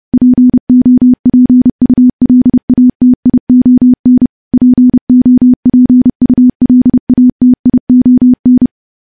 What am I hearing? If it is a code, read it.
Morse code, heard as a keyed tone, about 30 wpm, 250 Hz. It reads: POPULATION POPULATION
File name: morse-training-30WPM